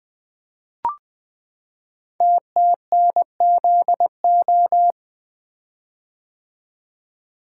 Morse code, heard as a keyed tone, about 20 wpm, 700 Hz. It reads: TTNZO
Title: Code Groups - 5 Letters - 20wpm